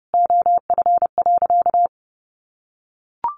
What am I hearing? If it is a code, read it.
Morse code, heard as a keyed tone, about 30 wpm, 700 Hz. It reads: OF.
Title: Easy Phrases 30wpm